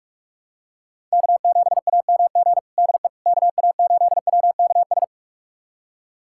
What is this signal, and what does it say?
Morse code, heard as a keyed tone, about 41 wpm, 700 Hz. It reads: K7AMG DE KA8WKS
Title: USA Callsign DE Callsign 45wpm